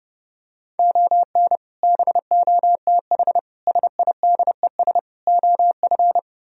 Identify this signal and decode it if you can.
Morse code, heard as a keyed tone, about 29 wpm, 700 Hz. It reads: ON BOTH SIDES OF